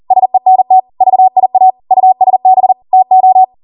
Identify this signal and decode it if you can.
Morse code, heard as a keyed tone, about 40 wpm, 750 Hz. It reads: SENT VIA USB TO